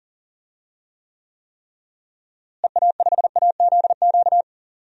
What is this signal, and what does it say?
Morse code, heard as a keyed tone, about 40 wpm, 700 Hz. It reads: EA5AZQ
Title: DXCC-100 Call Signs - 6 Characters - Repeated 3x - 40wpm